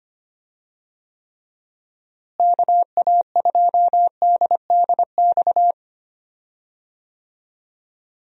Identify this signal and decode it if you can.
Morse code, heard as a keyed tone, about 25 wpm, 700 Hz. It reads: KA2DDX
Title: USA Call Signs - 6 Characters - Repeated 2x - 25wpm